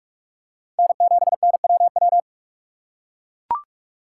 Morse code, heard as a keyed tone, about 41 wpm, 700 Hz. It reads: N7NWW
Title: USA Call Signs - 5 Characters - 4x Character Spacing - 45wpm